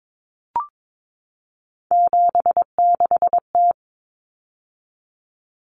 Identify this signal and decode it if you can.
Morse code, heard as a keyed tone, about 22 wpm, 700 Hz. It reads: 76T
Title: Common Cut Numbers - 3-Digit 22wpm